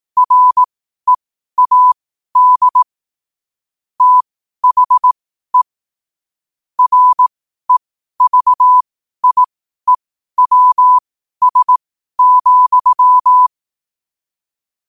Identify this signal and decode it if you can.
Morse code, heard as a keyed tone, about 18 wpm, spaced as for 12 wpm, 1000 Hz. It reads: READ THE REVIEWS,
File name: qotdfiles / 12wpm